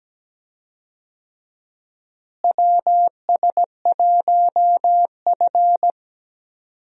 Morse code, heard as a keyed tone, about 17 wpm, 700 Hz. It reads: WS1F
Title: USA Call Signs - 4 Characters - Repeated 2x - 17wpm